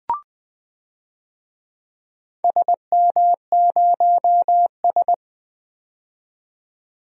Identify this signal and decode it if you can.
Morse code, heard as a keyed tone, about 21 wpm, 700 Hz. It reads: SM0S
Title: DXCC-100 Call Signs - 4 Characters - 20wpm